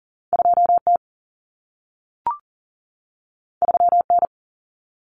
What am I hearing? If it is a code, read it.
Morse code, heard as a keyed tone, about 40 wpm, 700 Hz. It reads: 2T 3N
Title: Common Cut Numbers - 2-Digit 40wpm